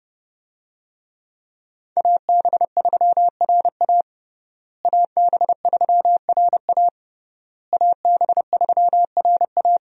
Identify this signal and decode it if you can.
Morse code, heard as a keyed tone, about 30 wpm, 700 Hz. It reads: AB3RA AB3RA AB3RA
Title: USA Call Signs - 5 Characters - Repeated 3x - 30wpm